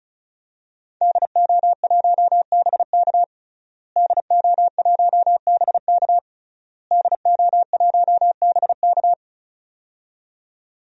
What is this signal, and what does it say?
Morse code, heard as a keyed tone, about 36 wpm, 700 Hz. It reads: DO1BK DO1BK DO1BK
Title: DXCC-100 Call Signs - 5 Characters - Repeated 3x - 35wpm